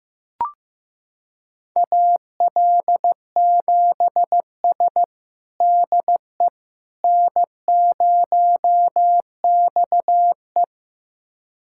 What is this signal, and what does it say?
Morse code, heard as a keyed tone, about 15 wpm, 700 Hz. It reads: AL7S DE N0XE
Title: USA Callsign DE Callsign - 4 Characters - 15wpm